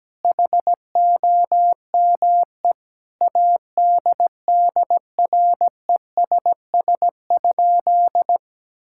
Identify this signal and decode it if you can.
Morse code, heard as a keyed tone, about 17 wpm, 700 Hz. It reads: HOME ADDRESS?
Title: Sentences from Top 1000 Words - Review Part 6 - 17wpm